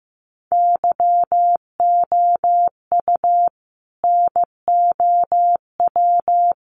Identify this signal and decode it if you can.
Morse code, heard as a keyed tone, about 15 wpm, 700 Hz. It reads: YOU NOW
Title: Sentences from Top 1100 Words 15wpm